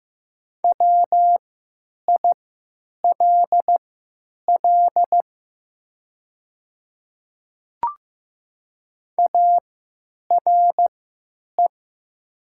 Morse code, heard as a keyed tone, about 15 wpm, spaced as for 9 wpm, 690 Hz. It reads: WILL ARE
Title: 3x Character Spacing - Top 100 Words - 15wpm